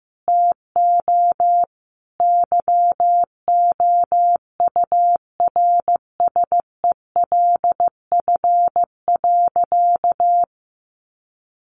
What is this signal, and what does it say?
Morse code, heard as a keyed tone, about 15 wpm, 700 Hz. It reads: TO YOURSELF.
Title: Sentences from Top 1100 Words 15wpm